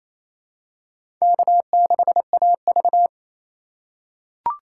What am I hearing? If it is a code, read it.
Morse code, heard as a keyed tone, about 28 wpm, 700 Hz. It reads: K6AV
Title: USA Call Signs - 4 Characters - 2x Character Spacing - 28wpm